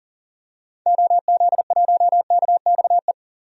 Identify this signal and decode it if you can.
Morse code, heard as a keyed tone, about 40 wpm, 700 Hz. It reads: OZ1KXE
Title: DXCC-100 Call Signs - 6 Characters - 40wpm